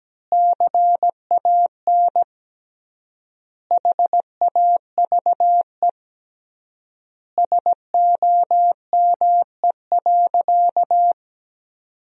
Morse code, heard as a keyed tone, about 17 wpm, 700 Hz. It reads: CAN HAVE SOME.
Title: 3x Word-Spacing - Sentences from Top 100 Words - 17wpm